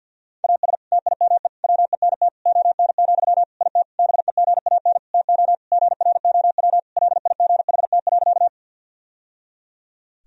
Morse code, heard as a keyed tone, about 50 wpm, 700 Hz. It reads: AS TIME WENT ON, IT BEGAN TO GROW LIGHT.
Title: Sentences from Top 1000 Words - Review Part 1 - 50wpm